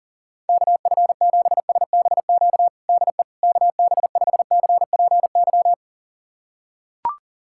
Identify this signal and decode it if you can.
Morse code, heard as a keyed tone, about 40 wpm, 700 Hz. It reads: KF7SBQ DE KB5CPY
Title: USA Callsign DE Callsign 40wpm